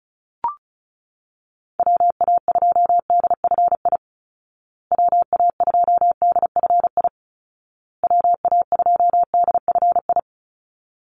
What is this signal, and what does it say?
Morse code, heard as a keyed tone, about 34 wpm, 700 Hz. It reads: WA2DFI WA2DFI WA2DFI
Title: Call Signs Repeated 2x - 35wpm